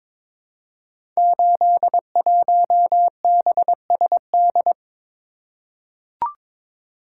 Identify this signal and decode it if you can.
Morse code, heard as a keyed tone, about 21 wpm, 700 Hz. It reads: 81BSD
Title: Code Groups - 5 Letter:Numbers - 22wpm